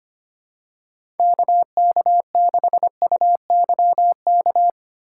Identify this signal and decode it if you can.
Morse code, heard as a keyed tone, about 25 wpm, 700 Hz. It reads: KK6UYK
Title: USA Call Signs - 6x Character Spacing - 25wpm